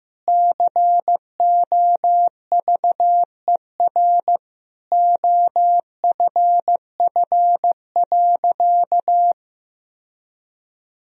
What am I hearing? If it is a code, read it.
Morse code, heard as a keyed tone, about 15 wpm, 700 Hz. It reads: COVER OFF.
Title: Sentences from Top 1000 Words - Review Part 1 - 15wpm